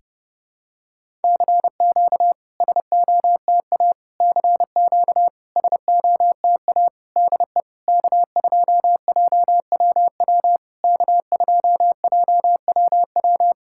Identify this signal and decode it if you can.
Morse code, heard as a keyed tone, about 30 wpm, 700 Hz. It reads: CQ SOTA CQ SOTA DE K2JWW K2JWW
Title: SOTA QSO Practice 30wpm